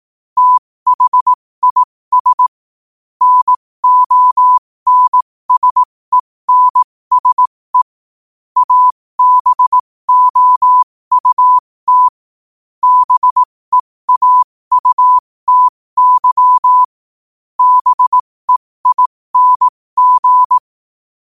Morse code, heard as a keyed tone, about 18 wpm, spaced as for 15 wpm, 1000 Hz. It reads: THIS NONSENSE ABOUT BEAUTY BEING
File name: qotdfiles / 15wpm